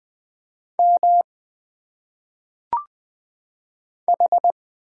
Morse code, heard as a keyed tone, about 20 wpm, 700 Hz. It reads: M H
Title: Single Letters - Phonetic Pronunciation 20wpm